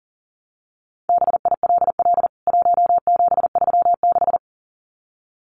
Morse code, heard as a keyed tone, about 40 wpm, 700 Hz. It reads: BILL 1736
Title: CWT Exchange 40wpm